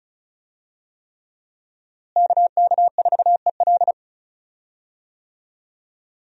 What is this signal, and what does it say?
Morse code, heard as a keyed tone, about 34 wpm, 700 Hz. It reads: KK4EL
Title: USA Call Signs - 5 Characters - Repeated 3x - 35wpm